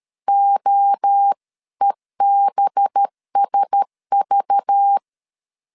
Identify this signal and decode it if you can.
Morse code, heard as a keyed tone, about 13 wpm, 800 Hz. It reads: OEBSV
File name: LEZIONI / 6 PKX0 / 6 P 12wpm 2033